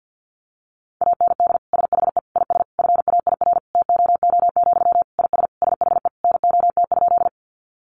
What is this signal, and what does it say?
Morse code, heard as a keyed tone, about 51 wpm, 700 Hz. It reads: AND SHE IS FAIR TOO, IS SHE NOT?
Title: Sentences from Top 1300 Words 50wpm